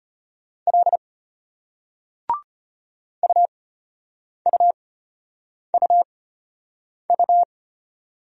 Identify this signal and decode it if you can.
Morse code, heard as a keyed tone, about 34 wpm, spaced as for 8 wpm, 700 Hz. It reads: L UUUU